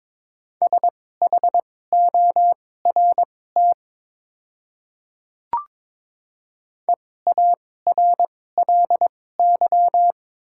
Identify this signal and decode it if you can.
Morse code, heard as a keyed tone, about 22 wpm, spaced as for 16 wpm, 700 Hz. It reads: SHORT EARLY
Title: 2x Character Spacing - Top 500 Words - 22wpm